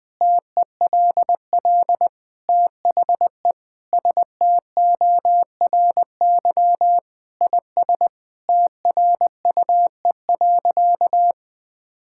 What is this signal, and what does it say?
Morse code, heard as a keyed tone, about 20 wpm, 700 Hz. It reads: TELL THE STORY IS TRUE.
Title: Sentences with 7-Word Phrases - 20wpm